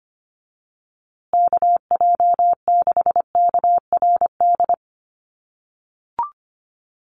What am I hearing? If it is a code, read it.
Morse code, heard as a keyed tone, about 25 wpm, 700 Hz. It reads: KJ6KRD
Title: USA Call Signs Repeated 2x 25wpm